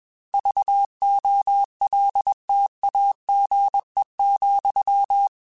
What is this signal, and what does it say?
Morse code, heard as a keyed tone, about 21 wpm, 800 Hz. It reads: VOLTAGE,
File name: Technical Subjects / Technical Subjects-mp3s / XC @20